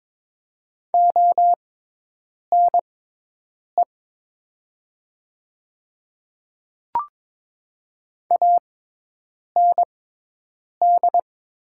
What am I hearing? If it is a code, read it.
Morse code, heard as a keyed tone, about 22 wpm, spaced as for 8 wpm, 700 Hz. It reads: ONE AND
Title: 6x Character Spacing - Top 100 Words - 22wpm